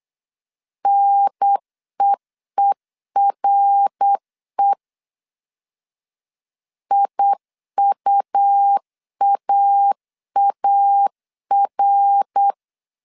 Morse code, heard as a keyed tone, about 8 wpm, 800 Hz. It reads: NEERE IUAAR
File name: CW / DET / 1 3 1500C S B 8wpm 4141